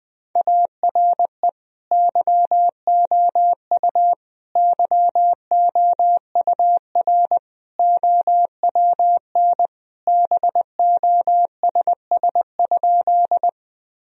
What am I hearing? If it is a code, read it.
Morse code, heard as a keyed tone, about 20 wpm, 700 Hz. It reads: ARE YOU YOUR OWN BOSS?
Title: Sentences from Top 1000 Words - Review Part 4 - 20wpm